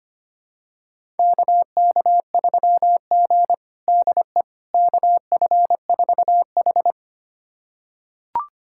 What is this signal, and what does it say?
Morse code, heard as a keyed tone, about 25 wpm, 700 Hz. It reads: KK3G DE KF4H
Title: USA Callsign DE Callsign - 4 Characters - 25wpm